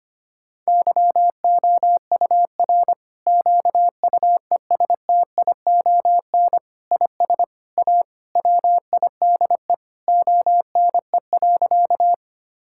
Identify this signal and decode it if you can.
Morse code, heard as a keyed tone, about 25 wpm, 700 Hz. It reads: YOUR QUESTION IS A WIDE ONE.
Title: Sentences from Top 1000 Words - Review Part 6 - 25wpm